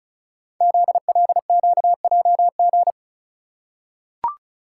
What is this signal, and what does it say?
Morse code, heard as a keyed tone, about 36 wpm, 690 Hz. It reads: ZLQJG
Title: Code Groups - 5 Letters - 35wpm